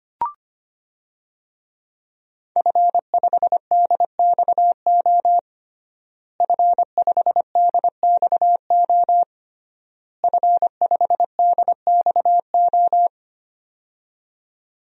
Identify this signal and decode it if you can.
Morse code, heard as a keyed tone, about 25 wpm, 700 Hz. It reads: F5DXO F5DXO F5DXO
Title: DXCC-100 Call Signs - 5 Characters - Repeated 3x - 25wpm